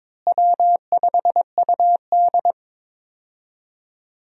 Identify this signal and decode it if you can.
Morse code, heard as a keyed tone, about 22 wpm, 700 Hz. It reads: W5UD
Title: USA Call Signs - 4 Characters - Repeated 3x - 22wpm